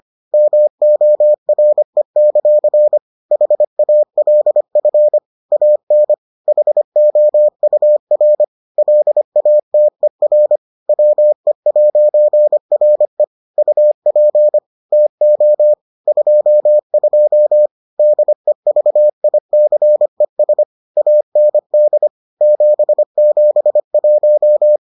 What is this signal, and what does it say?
Morse code, heard as a keyed tone, about 25 wpm, 600 Hz. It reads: MORE; HALF AN HOUR LATER WE'RE UP TO 22 DEVICES AND 771